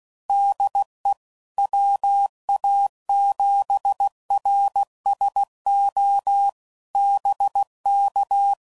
Morse code, heard as a keyed tone, about 16 wpm, 800 Hz. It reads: DE WA7RSO BK
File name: CW Code Groups @15WPM / QSOs / WD0FBT